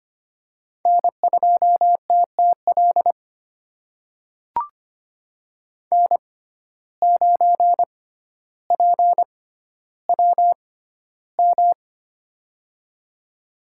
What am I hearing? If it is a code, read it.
Morse code, heard as a keyed tone, about 25 wpm, spaced as for 13 wpm, 700 Hz. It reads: N2TTL N9PWM
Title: USA Call Signs - 5 Characters - 6x Character Spacing - 25wpm